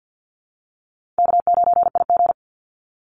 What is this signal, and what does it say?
Morse code, heard as a keyed tone, about 50 wpm, 700 Hz. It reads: K9IG
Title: Call Signs - 50wpm